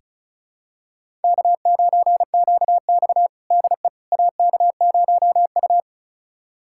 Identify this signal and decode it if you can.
Morse code, heard as a keyed tone, about 36 wpm, 700 Hz. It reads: K9QX DE AK0U
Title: USA Callsign DE Callsign - 4 Characters - 35wpm